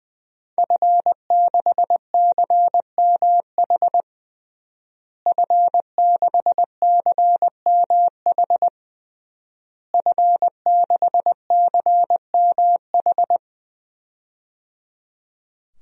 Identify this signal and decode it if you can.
Morse code, heard as a keyed tone, about 20 wpm, 700 Hz. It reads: F6CMH F6CMH F6CMH
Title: DXCC-100 Call Signs - 5 Characters - Repeated 3x - 20wpm